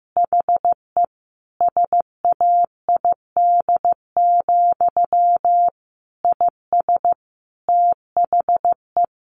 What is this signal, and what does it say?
Morse code, heard as a keyed tone, about 15 wpm, 700 Hz. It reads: HE SAID, IS THE
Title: A Sound Of Thunder 15wpm